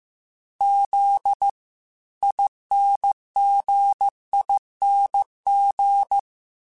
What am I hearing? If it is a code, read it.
Morse code, heard as a keyed tone, about 15 wpm, 800 Hz. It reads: Z INGING